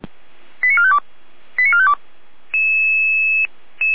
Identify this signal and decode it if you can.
Five-tone selective call (ZVEI), heard as quick runs of five tones, tones of about 70 ms. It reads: 89452 89452
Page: /455/SDR/ZVEI/